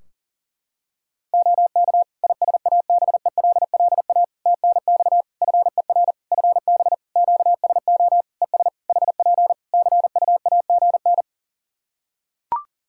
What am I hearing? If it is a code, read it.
Morse code, heard as a keyed tone, about 40 wpm, 700 Hz. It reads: OK ISABELLA TNX FER FB QSO ES HP CUAGN
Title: Practice QSO - Exchange 4 - Ending - 40wpm